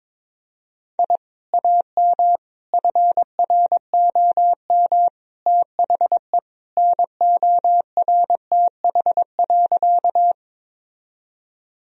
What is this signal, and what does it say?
Morse code, heard as a keyed tone, about 22 wpm, 700 Hz. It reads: I AM FROM THE NORTH.
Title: Sentences from Top 1000 Words - Review Part 1 - 22wpm